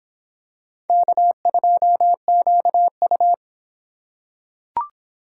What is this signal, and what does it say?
Morse code, heard as a keyed tone, about 24 wpm, 700 Hz. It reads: K2QU